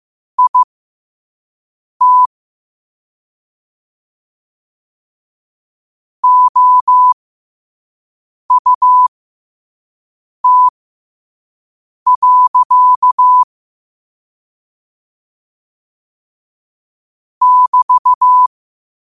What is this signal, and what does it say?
Morse code, heard as a keyed tone, about 15 wpm, spaced as for 5 wpm, 1000 Hz. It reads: IT OUT. <BT>
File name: qotdfiles / 5wpm